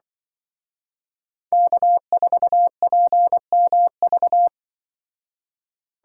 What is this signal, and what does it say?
Morse code, heard as a keyed tone, about 24 wpm, 700 Hz. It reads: K4PMV
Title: Speed Racing - USA 1x3 Call Signs 20wpm